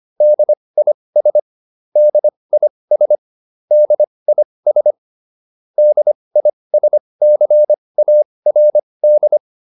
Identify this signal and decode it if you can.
Morse code, heard as a keyed tone, about 25 wpm, spaced as for 20 wpm, 600 Hz. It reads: DIS DIS DIS DISCARD